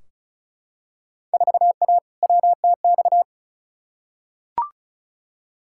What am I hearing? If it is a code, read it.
Morse code, heard as a keyed tone, about 36 wpm, 700 Hz. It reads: 4A WTX
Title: Field Day Exchanges Encore - 35wpm